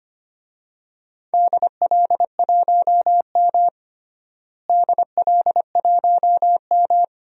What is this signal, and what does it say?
Morse code, heard as a keyed tone, about 25 wpm, 700 Hz. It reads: DL1M DL1M
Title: DXCC-100 Call Signs - 4 Characters - Repeated 2x - 25wpm